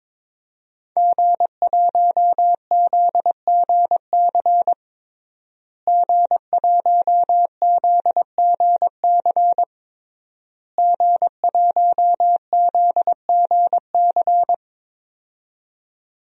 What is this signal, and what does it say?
Morse code, heard as a keyed tone, about 22 wpm, 700 Hz. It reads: G1ZGC G1ZGC G1ZGC
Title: DXCC-100 Call Signs - 5 Characters - Repeated 3x - 22wpm